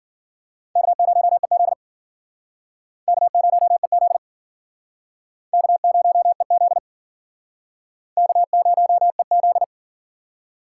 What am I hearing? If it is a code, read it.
Morse code, heard as a keyed tone, about 51 wpm, 700 Hz. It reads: K0EZ K0EZ K0EZ K0EZ